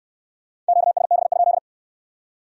Morse code, heard as a keyed tone, about 66 wpm, 700 Hz. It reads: QSB?